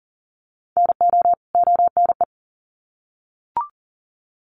Morse code, heard as a keyed tone, about 40 wpm, 700 Hz. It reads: NO ONE
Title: Sets of 2 Words 40wpm